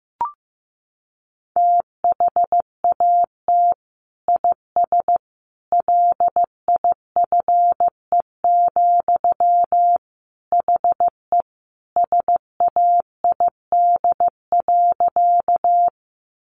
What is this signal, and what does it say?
Morse code, heard as a keyed tone, about 15 wpm, 700 Hz. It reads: THAT IS LIFE, HE SAID.
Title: Sentences from Top 900 Words 15wpm